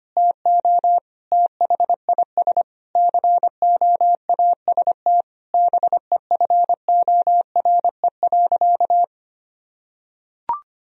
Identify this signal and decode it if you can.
Morse code, heard as a keyed tone, about 25 wpm, 700 Hz. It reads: TO THIS COAST BEFORE.
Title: Sentences from Top 1000 Words - Review Part 4 - 25wpm